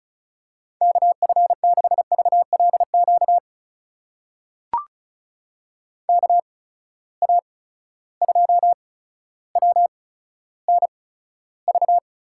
Morse code, heard as a keyed tone, about 36 wpm, spaced as for 15 wpm, 700 Hz. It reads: KF6VLQ KA2WNV
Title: USA Call Signs - 6 Characters - 8x Character Spacing - 35wpm